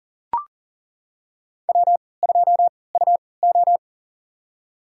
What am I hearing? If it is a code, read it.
Morse code, heard as a keyed tone, about 38 wpm, spaced as for 22 wpm, 700 Hz. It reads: W2UO
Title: USA Call Signs - 3x Character Spacing - 40wpm